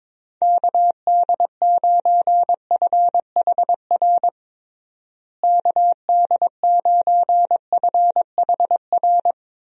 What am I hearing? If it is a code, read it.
Morse code, heard as a keyed tone, about 22 wpm, 700 Hz. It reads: KD9FHR KD9FHR
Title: USA Call Signs - 6 Characters - Repeated 2x - 22wpm